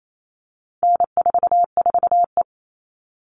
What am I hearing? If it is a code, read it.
Morse code, heard as a keyed tone, about 28 wpm, 700 Hz. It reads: N44E
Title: All Cut Numbers - 4-Digit 28wpm